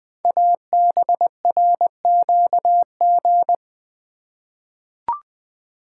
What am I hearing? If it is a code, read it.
Morse code, heard as a keyed tone, about 20 wpm, 700 Hz. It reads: ABRQG
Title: Code Groups - 5 Letters - 20wpm